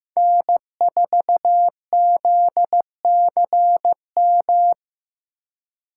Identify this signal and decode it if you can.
Morse code, heard as a keyed tone, about 15 wpm, 700 Hz. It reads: N4ZCM
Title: USA Call Signs - 5 Characters - 3x Character Spacing - 15wpm